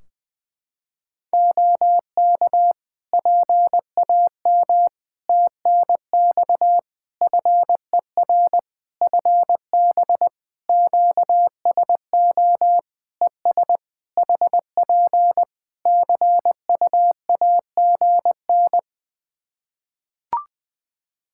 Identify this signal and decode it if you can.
Morse code, heard as a keyed tone, about 20 wpm, 700 Hz. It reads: OK PAM TNX FER FB QSO ES HP CUAGN
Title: K1USN SST QSO Practice 20wpm